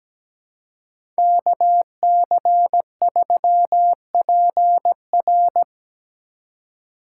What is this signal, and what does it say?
Morse code, heard as a keyed tone, about 17 wpm, 700 Hz. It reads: KC3PR
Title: USA Call Signs - 2x Character Spacing - 17wpm